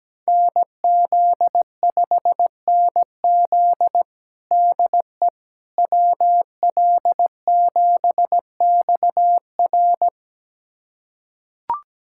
Morse code, heard as a keyed tone, about 17 wpm, 700 Hz. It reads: NZ5NZ DE WL7XR
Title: USA Callsign DE Callsign - 5 Characters - 17wpm